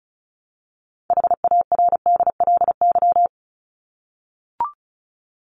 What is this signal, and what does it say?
Morse code, heard as a keyed tone, about 34 wpm, 700 Hz. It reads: HARDLY
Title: LY Suffix - Top 100 Words with Base Words 35wpm